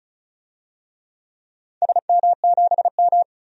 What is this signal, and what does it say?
Morse code, heard as a keyed tone, about 36 wpm, 700 Hz. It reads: SM7M
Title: DXCC-100 Call Signs - 4 Characters - 35wpm